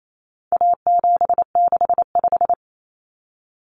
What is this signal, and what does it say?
Morse code, heard as a keyed tone, about 28 wpm, 700 Hz. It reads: A765
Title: Common Cut Numbers - 4-Digit 28wpm